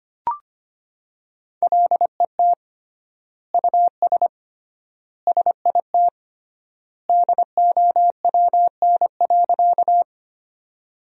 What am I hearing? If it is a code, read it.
Morse code, heard as a keyed tone, about 25 wpm, 700 Hz. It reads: LET US SIT DOWN.
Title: 3x Word-Spacing - Sentences from Top 500 Words - 25wpm